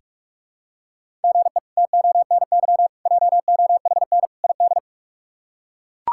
Morse code, heard as a keyed tone, about 41 wpm, 700 Hz. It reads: GE TONY JOHN ID
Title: K1USN SST QSO Practice - 45wpm